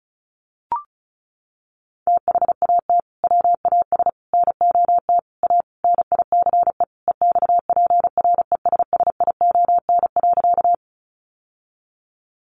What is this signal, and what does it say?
Morse code, heard as a keyed tone, about 34 wpm, 700 Hz. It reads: THAT WAS NOT A NICE EXPRESSION.
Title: Sentences from Top 1200 Words 35wpm